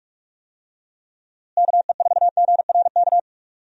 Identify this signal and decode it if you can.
Morse code, heard as a keyed tone, about 46 wpm, 700 Hz. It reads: KE4GRK
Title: USA Call Signs - 6 Characters - Repeated 2x - 45wpm